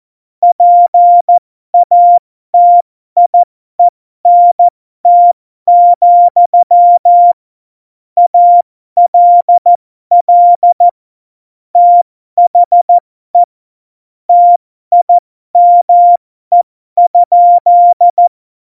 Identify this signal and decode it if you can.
Morse code, heard as a keyed tone, about 14 wpm, spaced as for 12 wpm, 700 Hz. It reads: PATIENT, ALL THE TIME?
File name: Morsekurs / Witze